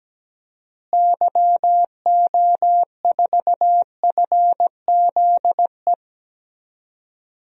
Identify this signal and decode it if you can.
Morse code, heard as a keyed tone, about 17 wpm, 700 Hz. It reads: YO4FZE
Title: DXCC-100 Call Signs - 6 Characters - Repeated 2x - 17wpm